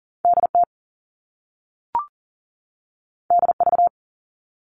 Morse code, heard as a keyed tone, about 41 wpm, 700 Hz. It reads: DT DV
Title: All Cut Numbers - 2-Digit 40wpm